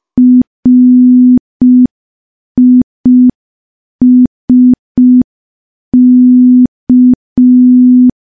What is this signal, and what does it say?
Morse code, heard as a keyed tone, about 5 wpm, 250 Hz. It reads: RISK
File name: morse / morse-training-05WPM